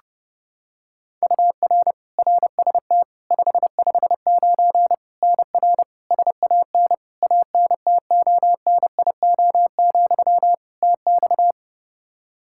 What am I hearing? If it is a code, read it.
Morse code, heard as a keyed tone, about 30 wpm, 700 Hz. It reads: UR RST 559 NR SAN ANTONIO, TX
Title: K1USN SST QSO Practice 30wpm